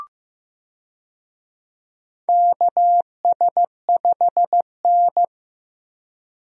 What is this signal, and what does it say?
Morse code, heard as a keyed tone, about 15 wpm, 700 Hz. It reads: KS5N
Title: USA Call Signs - 4 Characters - Repeated 3x - 15wpm